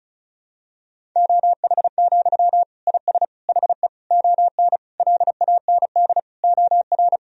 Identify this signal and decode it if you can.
Morse code, heard as a keyed tone, about 35 wpm, 700 Hz. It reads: OH, IS HE ON LAND OR
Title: Sentences from Top 1000 Words - Review Part 1 - 35wpm